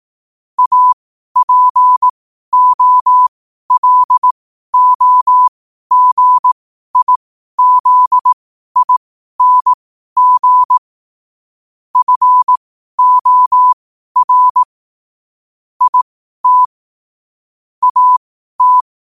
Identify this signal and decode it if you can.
Morse code, heard as a keyed tone, about 18 wpm, spaced as for 12 wpm, 1000 Hz. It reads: APOLOGIZING FOR IT AT
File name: qotdfiles / 12wpm